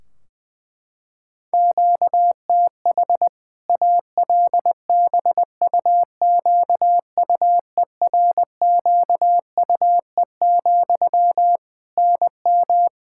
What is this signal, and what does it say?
Morse code, heard as a keyed tone, about 21 wpm, 700 Hz. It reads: QTH ALBUQUERQUE, NM
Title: K1USN SST QSO Practice 20wpm